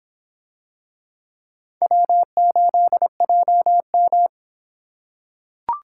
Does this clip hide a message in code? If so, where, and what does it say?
Yes: Morse code from 1.8 to 4.3 s, heard as a keyed tone, about 24 wpm, 700 Hz. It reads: W8JM